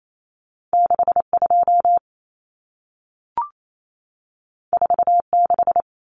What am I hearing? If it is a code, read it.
Morse code, heard as a keyed tone, about 28 wpm, 700 Hz. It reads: 62 46
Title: Numbers - 2-Digit 28wpm